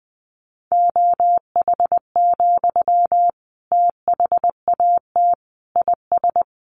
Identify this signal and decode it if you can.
Morse code, heard as a keyed tone, about 20 wpm, 700 Hz. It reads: OH, THAT IS
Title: Sentences from Top 700 Words 20wpm